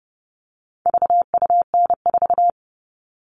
Morse code, heard as a keyed tone, about 30 wpm, 700 Hz. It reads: VUN4
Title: All Cut Numbers - 4-Digit 30wpm